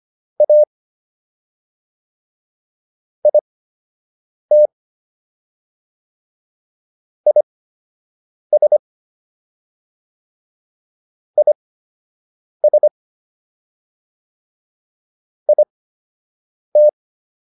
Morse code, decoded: A IT IS IS IT

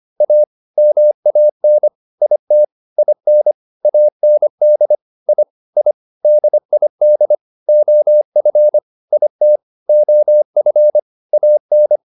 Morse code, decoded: A MAN IT IN AND I I DID OF IT OF AN

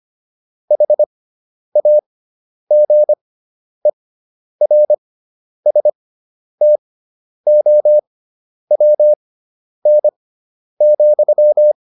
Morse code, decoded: HAGERSTOWN,

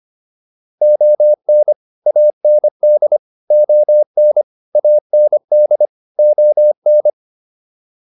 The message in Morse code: ON AND ON AND ON

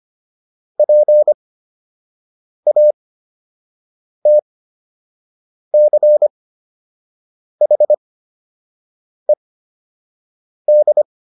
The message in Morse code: PATCHED

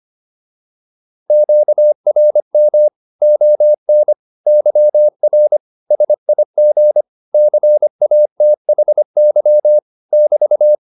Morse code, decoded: QRM ON YR SIG CATHY <BT>

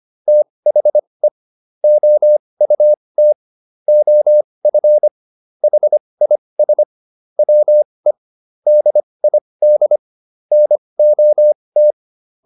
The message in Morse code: THE OUT OF HIS WE DID NOT

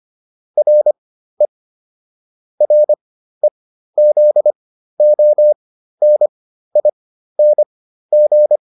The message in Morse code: RE REZONING

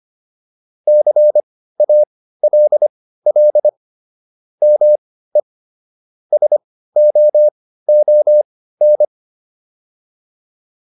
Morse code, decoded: CALL ME SOON

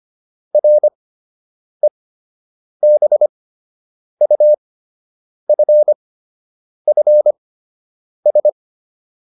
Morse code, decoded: REBUFFS

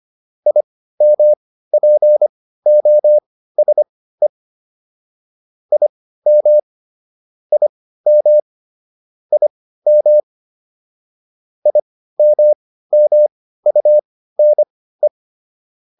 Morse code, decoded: IMPOSE IM IM IM IMMUNE

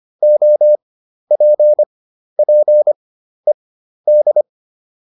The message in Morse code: OPPED

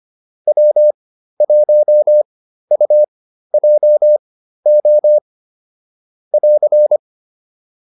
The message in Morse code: W1UJO <AR>

